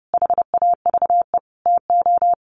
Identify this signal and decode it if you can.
Morse code, heard as a keyed tone, about 30 wpm, 700 Hz. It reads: HAVE TO
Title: Sets of 5 Words 30wpm